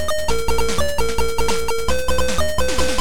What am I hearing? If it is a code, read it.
There is no code.